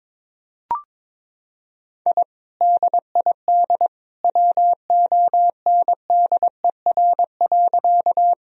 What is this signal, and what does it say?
Morse code, heard as a keyed tone, about 22 wpm, 700 Hz. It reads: I DID WONDER.
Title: Sentences from Top 1000 Words - Review Part 3 - 22wpm